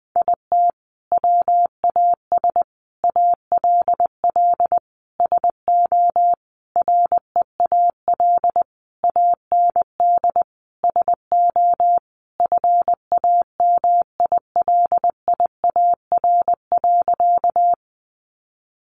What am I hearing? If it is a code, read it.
Morse code, heard as a keyed tone, about 20 wpm, 700 Hz. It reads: IT WAS ALL SO REAL AND SO FAMILIAR.
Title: Sentences from Top 1000 Words 20wpm